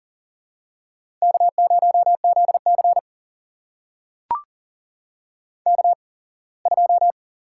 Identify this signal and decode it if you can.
Morse code, heard as a keyed tone, about 40 wpm, spaced as for 24 wpm, 700 Hz. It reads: K0ZC K2
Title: USA Call Signs - 4 Characters - 8x Character Spacing - 40wpm